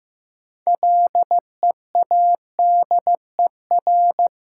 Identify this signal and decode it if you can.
Morse code, heard as a keyed tone, about 15 wpm, 700 Hz. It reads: LEADER
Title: Six Letter Words - 15wpm